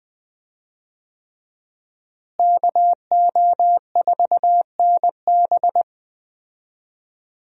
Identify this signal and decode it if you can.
Morse code, heard as a keyed tone, about 19 wpm, 700 Hz. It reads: KO4NB
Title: USA Call Signs - 5 Characters - Repeated 3x - 20wpm